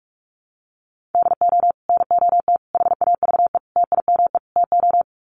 Morse code, heard as a keyed tone, about 44 wpm, 700 Hz. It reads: DO NOT HAVE TIME TO
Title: Sets of 5 Words 45wpm